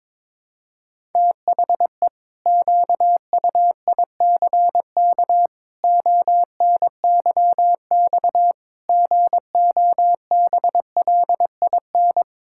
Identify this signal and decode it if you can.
Morse code, heard as a keyed tone, about 22 wpm, 700 Hz. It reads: THE QUICK ONYX GOBLIN
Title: Pangram Challenge v2 - 22wpm